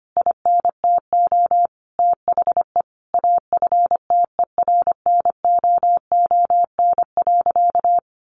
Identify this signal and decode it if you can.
Morse code, heard as a keyed tone, about 25 wpm, 700 Hz. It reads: INTO THE AFTERNOON.
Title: Sentences from Top 1500 Words 25wpm